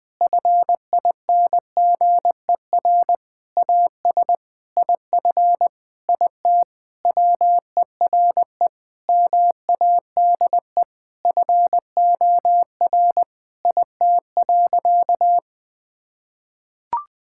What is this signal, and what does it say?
Morse code, heard as a keyed tone, about 20 wpm, 700 Hz. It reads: FINGER AS IF IT WERE MADE FOR IT.
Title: Sentences from Top 1600 Words 20wpm